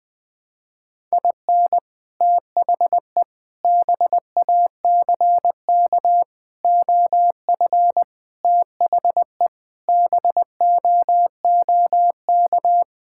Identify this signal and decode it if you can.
Morse code, heard as a keyed tone, about 20 wpm, 700 Hz. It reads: IN THE BACK OF THE BOOK
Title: Sentences with 5-Word Phrases - 20wpm